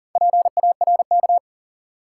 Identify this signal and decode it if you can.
Morse code, heard as a keyed tone, about 38 wpm, 700 Hz. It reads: PARK
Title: Sentences from Top 1300 Words - New Word Drill Alt - 40wpm